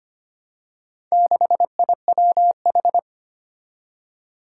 Morse code, heard as a keyed tone, about 25 wpm, 700 Hz. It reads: 6IWH
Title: Four Challenging Letter-Number Combinations 25wpm